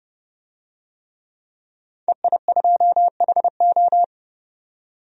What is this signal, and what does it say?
Morse code, heard as a keyed tone, about 31 wpm, 700 Hz. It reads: EI2HO
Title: DXCC-100 Call Signs - 5 Characters - 30wpm